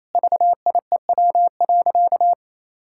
Morse code, heard as a keyed tone, about 28 wpm, 700 Hz. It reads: VIEW.
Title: Sentences from Top 1000 Words - Review Part 6 - 28wpm